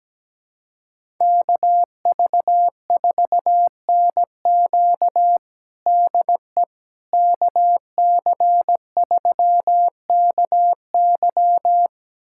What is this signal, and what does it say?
Morse code, heard as a keyed tone, about 17 wpm, 700 Hz. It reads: KV4NQ DE KC3KY
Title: USA Callsign DE Callsign 17wpm